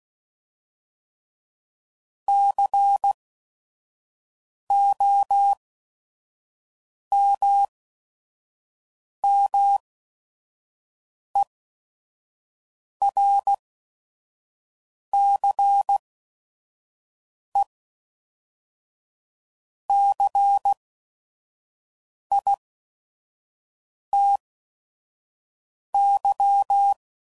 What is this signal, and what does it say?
Morse code, heard as a keyed tone, about 16 wpm, spaced as for 5 wpm, 800 Hz. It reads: COMMERCE CITY
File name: CW Code Groups @5WPM / fromSM's / Cities